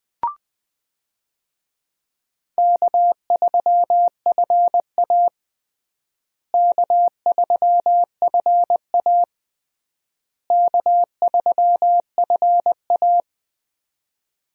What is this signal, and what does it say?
Morse code, heard as a keyed tone, about 20 wpm, 700 Hz. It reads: K3FA K3FA K3FA
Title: USA Call Signs - 4 Characters - Repeated 3x - 20wpm